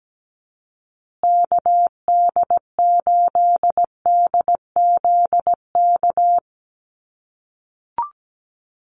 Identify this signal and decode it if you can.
Morse code, heard as a keyed tone, about 17 wpm, 700 Hz. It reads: KD8DZK
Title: USA Call Signs Repeated 2x 17wpm